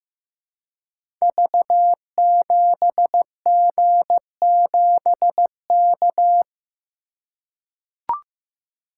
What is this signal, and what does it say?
Morse code, heard as a keyed tone, about 15 wpm, 700 Hz. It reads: V7G7K
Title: Code Groups - 5 Letter:Numbers - 15wpm